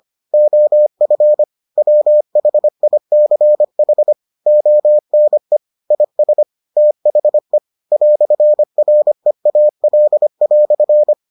Morse code, heard as a keyed tone, about 25 wpm, 600 Hz. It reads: OF WHICH ONE IS THE "REAL"